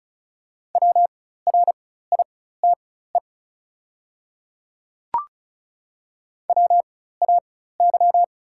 Morse code, heard as a keyed tone, about 36 wpm, spaced as for 16 wpm, 690 Hz. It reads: WRITE WAY
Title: 4x Character Spacing - Top 100 Words - 35wpm